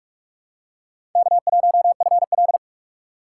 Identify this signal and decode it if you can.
Morse code, heard as a keyed tone, about 41 wpm, 700 Hz. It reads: K1FL
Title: USA Call Signs - 2x Character Spacing - 45wpm